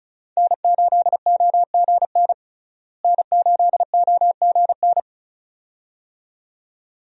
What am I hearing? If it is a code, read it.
Morse code, heard as a keyed tone, about 34 wpm, 700 Hz. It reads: N8OGN N8OGN
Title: USA Call Signs - 5 Characters - Repeated 2x - 35wpm